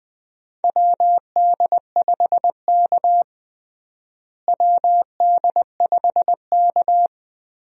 Morse code, heard as a keyed tone, about 20 wpm, 700 Hz. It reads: WD5K WD5K
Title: USA Call Signs - 4 Characters - Repeated 2x - 20wpm